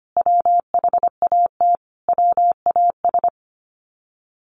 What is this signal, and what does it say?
Morse code, heard as a keyed tone, about 25 wpm, 700 Hz. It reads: WHAT WAS
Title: Sets of 5 Words Encore2 25wpm